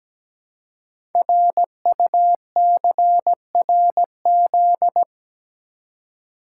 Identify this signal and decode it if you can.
Morse code, heard as a keyed tone, about 17 wpm, 700 Hz. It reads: RUCRZ
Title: Code Groups - 5 Letters - 17wpm